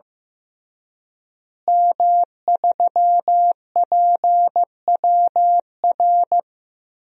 Morse code, heard as a keyed tone, about 15 wpm, 700 Hz. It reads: M3PWR
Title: DXCC-100 Call Signs - 5 Characters - Repeated 2x - 15wpm